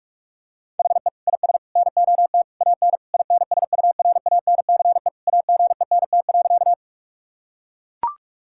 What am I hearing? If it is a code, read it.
Morse code, heard as a keyed tone, about 41 wpm, 700 Hz. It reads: HE IS NOT AN INSURANCE AGENT.
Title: Sentences from Top 1600 Words 45wpm